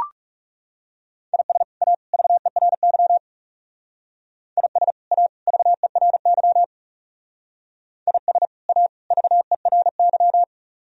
Morse code, heard as a keyed tone, about 36 wpm, 700 Hz. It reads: IS A VERY IS A VERY IS A VERY
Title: Speed-Racing - Sets of 3 Words 30wpm